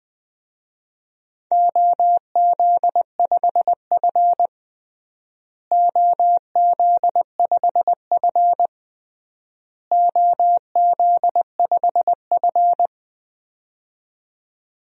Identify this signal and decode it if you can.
Morse code, heard as a keyed tone, about 21 wpm, 700 Hz. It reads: OZ5F OZ5F OZ5F
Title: DXCC-100 Call Signs - 4 Characters - Repeated 3x - 20wpm